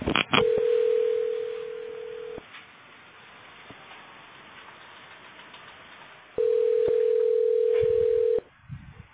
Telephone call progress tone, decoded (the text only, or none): ringback tone